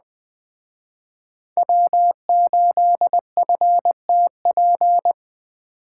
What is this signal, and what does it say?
Morse code, heard as a keyed tone, about 20 wpm, 700 Hz. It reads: W8FTP
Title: Speed Racing - USA 1x3 Call Signs 17wpm